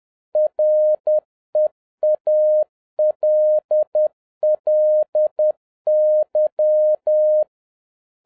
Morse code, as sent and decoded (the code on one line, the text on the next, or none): .-. .- .-.. .-.. -.--
REALLY